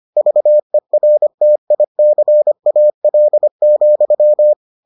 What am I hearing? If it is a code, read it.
Morse code, heard as a keyed tone, about 25 wpm, 600 Hz. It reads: VERTICAL,